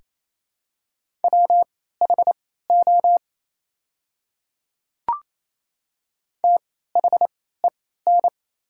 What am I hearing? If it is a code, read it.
Morse code, heard as a keyed tone, about 28 wpm, spaced as for 16 wpm, 700 Hz. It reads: WHO THEN
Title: 3x Character Spacing - Top 100 Words - 28wpm